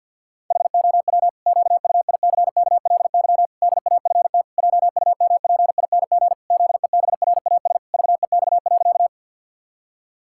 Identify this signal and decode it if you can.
Morse code, heard as a keyed tone, about 46 wpm, 700 Hz. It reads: HOW QUICKLY DAFT JUMPING ZEBRAS VEX.
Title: Pangram Challenge v2 - 50wpm